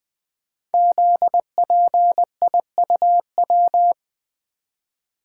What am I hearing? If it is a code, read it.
Morse code, heard as a keyed tone, about 20 wpm, 700 Hz. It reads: ZPIUW
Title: Code Groups - 5 Letters - 20wpm